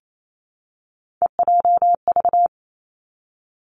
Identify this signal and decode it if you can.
Morse code, heard as a keyed tone, about 28 wpm, 700 Hz. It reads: EJV
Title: Three Challenging Letter-Number Combinations 28wpm